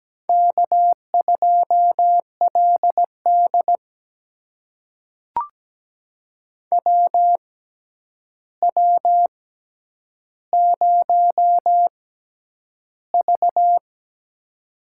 Morse code, heard as a keyed tone, about 17 wpm, spaced as for 9 wpm, 700 Hz. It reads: K2LD WW0V
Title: USA Call Signs - 4 Characters - 6x Character Spacing - 17wpm